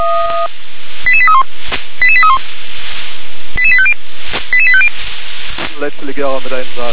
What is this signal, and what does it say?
Five-tone selective call (ZVEI), heard as quick runs of five tones, tones of about 70 ms, with speech. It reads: 88831 88831 88850 88850